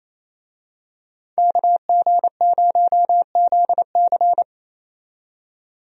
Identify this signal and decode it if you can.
Morse code, heard as a keyed tone, about 28 wpm, 700 Hz. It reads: KG0ZC
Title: USA Call Signs - 5 Characters - 28wpm